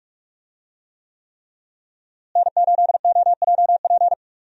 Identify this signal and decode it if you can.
Morse code, heard as a keyed tone, about 46 wpm, 700 Hz. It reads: N8OJP
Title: USA Call Signs - 5 Characters - Repeated 3x - 45wpm